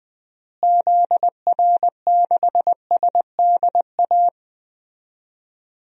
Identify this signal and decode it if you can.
Morse code, heard as a keyed tone, about 21 wpm, 700 Hz. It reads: ZR6SDA
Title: DXCC-100 Call Signs - 6 Characters - Repeated 3x - 20wpm